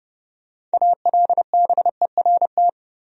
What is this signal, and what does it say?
Morse code, heard as a keyed tone, about 30 wpm, 700 Hz. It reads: ALBERT
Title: USA Common Names - 30wpm